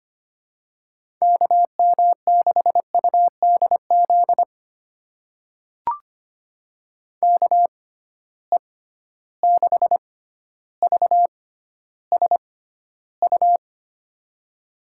Morse code, heard as a keyed tone, about 25 wpm, spaced as for 13 wpm, 700 Hz. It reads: KM6UDZ KE6VSU
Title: USA Call Signs - 6x Character Spacing - 25wpm